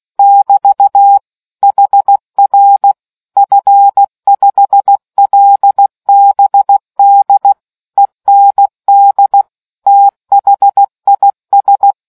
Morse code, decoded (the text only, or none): <BT> HR F5LBD END THIS